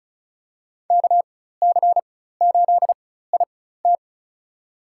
Morse code, heard as a keyed tone, about 34 wpm, spaced as for 16 wpm, 700 Hz. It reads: KC8IT
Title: USA Call Signs - 4x Character Spacing - 35wpm